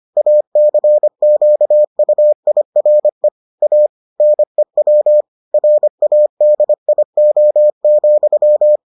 Morse code, heard as a keyed tone, about 25 wpm, 600 Hz. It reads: ACQUIRE A NEW RADIO,